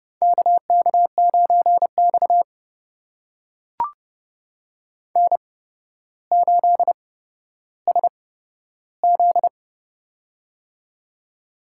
Morse code, heard as a keyed tone, about 30 wpm, spaced as for 13 wpm, 700 Hz. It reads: KK9X N8SZ
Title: USA Call Signs - 4 Characters - 8x Character Spacing - 30wpm